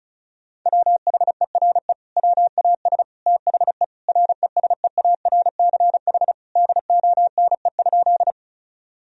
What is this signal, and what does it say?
Morse code, heard as a keyed tone, about 34 wpm, 700 Hz. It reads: WHERE WAS THE RESEARCH DONE?
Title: Sentences from Top 1000 Words - Review Part 4 - 35wpm